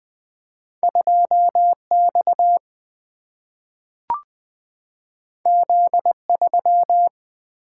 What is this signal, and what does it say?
Morse code, heard as a keyed tone, about 20 wpm, 700 Hz. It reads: 2X Z3
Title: Single-Letter Single-Number 20wpm